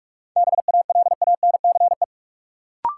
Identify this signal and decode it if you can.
Morse code, heard as a keyed tone, about 46 wpm, 700 Hz. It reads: BALANCE
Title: Sentences from Top 900 Words - New Word Drill - 45wpm